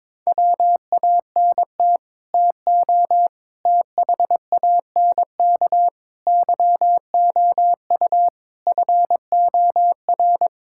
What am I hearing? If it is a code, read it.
Morse code, heard as a keyed tone, about 22 wpm, 700 Hz. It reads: WANT TO THANK YOU FOR
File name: practice / 22wpm / Sets of 8 Words - 22wpm